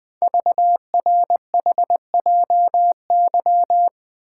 Morse code, heard as a keyed tone, about 20 wpm, 700 Hz. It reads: VRHJY
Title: Code Groups - 5 Letter:Numbers - 20wpm